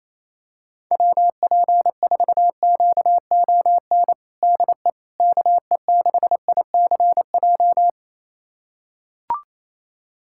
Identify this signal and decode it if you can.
Morse code, heard as a keyed tone, about 28 wpm, 700 Hz. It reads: WP4QON DE KE6ICJ
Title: USA Callsign DE Callsign 28wpm